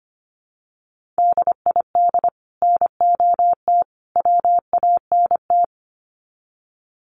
Morse code, heard as a keyed tone, about 25 wpm, 700 Hz. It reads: DID NOT WANT
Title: Sets of 3 Words 25wpm